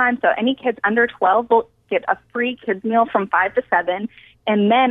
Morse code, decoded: none